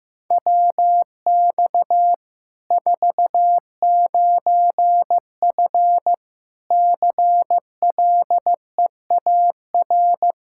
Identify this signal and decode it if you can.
Morse code, heard as a keyed tone, about 15 wpm, 700 Hz. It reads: WX 49F CLEAR
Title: Practice QSO - Exchange 2 - Info - 15wpm